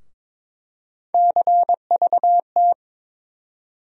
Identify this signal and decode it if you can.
Morse code, heard as a keyed tone, about 22 wpm, 700 Hz. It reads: CVT
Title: USA Call Sign Postfixes - 3 Letters 22wpm